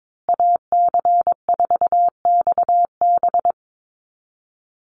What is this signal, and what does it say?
Morse code, heard as a keyed tone, about 22 wpm, 700 Hz. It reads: AC4XB
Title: USA Call Signs Repeated 2x 22wpm